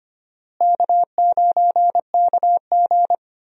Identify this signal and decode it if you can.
Morse code, heard as a keyed tone, about 25 wpm, 700 Hz. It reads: K9KG
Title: USA Call Signs - 4 Characters - Repeated 3x - 25wpm